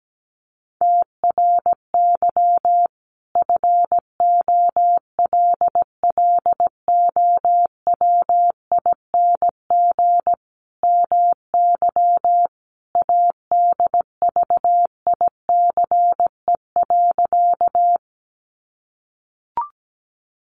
Morse code, heard as a keyed tone, about 17 wpm, 700 Hz. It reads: TRY FOLLOWING MY ADVICE.
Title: Sentences from Top 1100 Words 17wpm